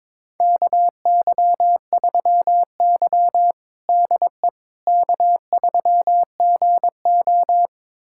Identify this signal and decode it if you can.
Morse code, heard as a keyed tone, about 21 wpm, 700 Hz. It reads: KY3Y DE K3GO
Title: USA Callsign DE Callsign - 4 Characters - 22wpm